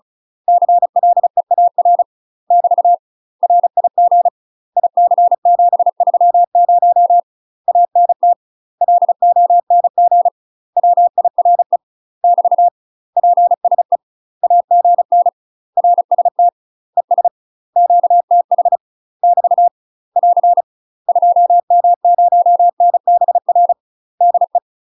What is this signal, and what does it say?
Morse code, heard as a keyed tone, about 34 wpm, 700 Hz. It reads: CLEAR <BT> RIG IC730 ANT LONG WIRE <BT> PSE AGN RST ES QTH <BT> <AR> 2M0NBR DE
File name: UK MASTER QSOs TRAINING/35WPM